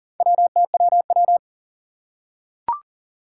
Morse code, heard as a keyed tone, about 40 wpm, 700 Hz. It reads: WTWW
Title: Four Challenging Letter-Number Combinations 40wpm